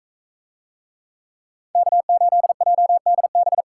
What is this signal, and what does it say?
Morse code, heard as a keyed tone, about 42 wpm, 700 Hz. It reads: K8JDB